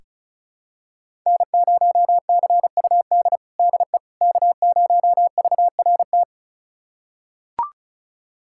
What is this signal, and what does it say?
Morse code, heard as a keyed tone, about 36 wpm, 700 Hz. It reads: N0CUD DE K0VRT
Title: USA Callsign DE Callsign - 5 Characters - 35wpm